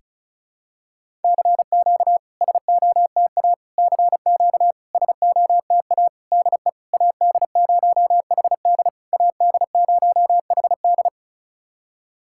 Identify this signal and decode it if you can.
Morse code, heard as a keyed tone, about 34 wpm, 690 Hz. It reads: CQ SOTA CQ SOTA DE AD0HD AD0HD
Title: SOTA QSO Practice 35wpm